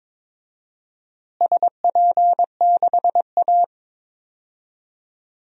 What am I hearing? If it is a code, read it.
Morse code, heard as a keyed tone, about 22 wpm, 700 Hz. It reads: SP6A
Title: DXCC-100 Call Signs - 4 Characters - Repeated 2x - 22wpm